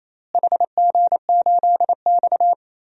Morse code, heard as a keyed tone, about 27 wpm, 700 Hz. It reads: HG8X
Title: DXCC-100 Call Signs - 4 Characters - Repeated 2x - 28wpm